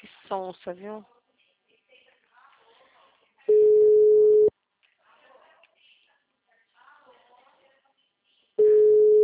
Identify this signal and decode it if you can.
Ringback tone (Europe, 425 Hz), with speech.